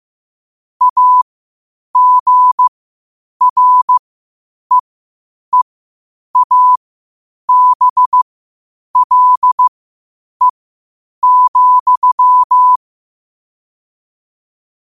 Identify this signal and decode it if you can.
Morse code, heard as a keyed tone, about 15 wpm, spaced as for 8 wpm, 1000 Hz. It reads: AGREEABLE,
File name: qotdfiles / 8wpm